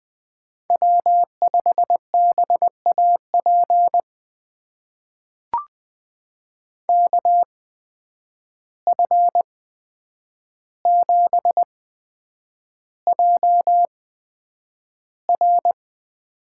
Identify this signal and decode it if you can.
Morse code, heard as a keyed tone, about 20 wpm, spaced as for 9 wpm, 700 Hz. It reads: W5BAP KF7JR
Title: USA Call Signs - 5 Characters - 8x Character Spacing - 20wpm